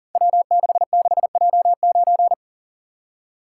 Morse code, heard as a keyed tone, about 42 wpm, 700 Hz. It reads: W66J9
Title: Code Groups - 5 Letter:Numbers - 40wpm